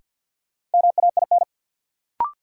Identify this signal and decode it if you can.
Morse code, heard as a keyed tone, about 53 wpm, 700 Hz. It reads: MAIN
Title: Sentences from Top 1300 Words - New Word Drill Alt - 50wpm